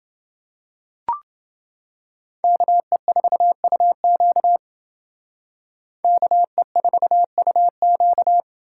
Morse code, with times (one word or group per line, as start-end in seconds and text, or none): 2.4-4.6 KE4UQ
6.0-8.4 KE4UQ